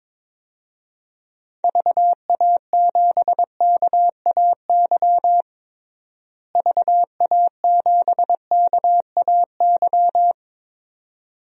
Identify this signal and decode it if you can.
Morse code, heard as a keyed tone, about 22 wpm, 700 Hz. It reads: VA7KAY VA7KAY
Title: DXCC-100 Call Signs - 6 Characters - Repeated 2x - 22wpm